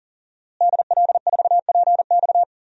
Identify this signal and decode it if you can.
Morse code, heard as a keyed tone, about 40 wpm, 700 Hz. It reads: DL4PX
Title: DXCC-100 Call Signs - 5 Characters - Repeated 3x - 40wpm